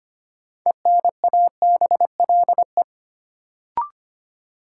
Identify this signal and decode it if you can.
Morse code, heard as a keyed tone, about 25 wpm, 700 Hz. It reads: ENABLE
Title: Sentences from Top 1500 Words - New Word Drill Alt - 25wpm